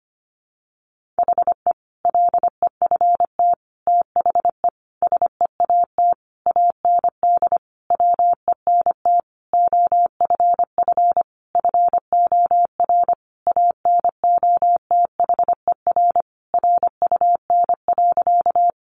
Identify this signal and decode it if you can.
Morse code, heard as a keyed tone, about 25 wpm, 700 Hz. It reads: HE LEFT THE SEAT AND WENT OFF FOR ANOTHER RUN.
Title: Sentences from Top 1500 Words 25wpm